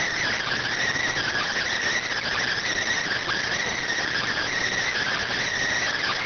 Emergency Alert System (SAME) tones, heard as harsh buzzing bursts. It is heard from 1.2 to 5.2 s.